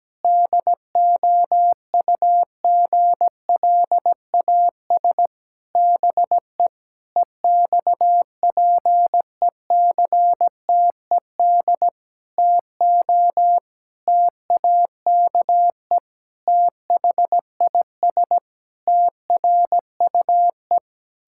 Morse code, heard as a keyed tone, about 17 wpm, 700 Hz. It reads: DOUGLAS BE EXPECTED TO TAKE THIS TRUE
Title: Pangram Challenge v2 - 17wpm